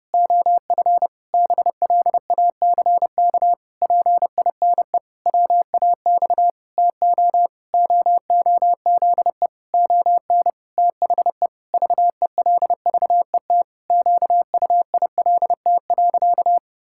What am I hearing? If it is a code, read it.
Morse code, heard as a keyed tone, about 30 wpm, 700 Hz. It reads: OF BLACK PINE WAX TO OOZE ON THE VELVET QUILT.
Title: Pangram Challenge v2 - 30wpm